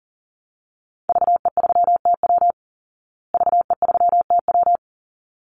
Morse code, heard as a keyed tone, about 40 wpm, 700 Hz. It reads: VE3TW VE3TW
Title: Call Signs Repeated 3x - 40wpm